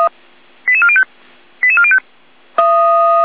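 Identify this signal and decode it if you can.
Five-tone selective call (ZVEI), heard as quick runs of five tones, tones of about 70 ms. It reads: 88405 88405
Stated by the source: ZVEI five-tone call received on SDR.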